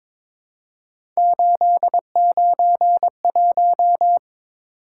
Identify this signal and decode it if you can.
Morse code, heard as a keyed tone, about 22 wpm, 700 Hz. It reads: 891
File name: practice / 22wpm / Three Letter-Number Combinations - 22wpm